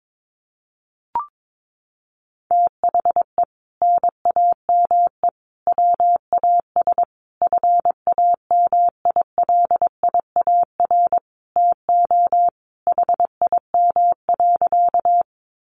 Morse code, heard as a keyed tone, about 22 wpm, 700 Hz. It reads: THE NAME WAS FAMILIAR TO HIM.
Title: Sentences from Top 1000 Words 22wpm